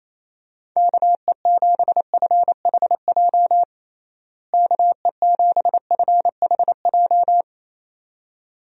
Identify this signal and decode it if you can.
Morse code, heard as a keyed tone, about 28 wpm, 700 Hz. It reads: KE7FHJ KE7FHJ
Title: USA Call Signs - 6 Characters - Repeated 2x - 28wpm